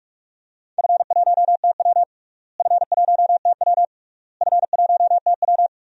Morse code, heard as a keyed tone, about 47 wpm, 700 Hz. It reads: F1TW F1TW F1TW
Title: DXCC-100 Call Signs - 4 Characters - Repeated 3x - 45wpm